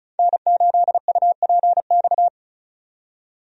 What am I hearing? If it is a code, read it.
Morse code, heard as a keyed tone, about 36 wpm, 700 Hz. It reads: N8UPX
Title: USA Call Signs - 5 Characters - 35wpm